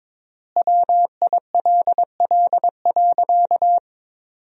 Morse code, heard as a keyed tone, about 22 wpm, 700 Hz. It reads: WILL.
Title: 4x Word-Spacing - Sentences from Top 200 Words - 22wpm